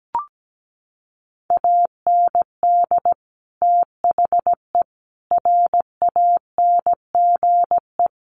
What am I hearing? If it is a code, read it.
Morse code, heard as a keyed tone, about 17 wpm, 700 Hz. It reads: AND THE RANGE
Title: Sentences from Top 1500 Words 17wpm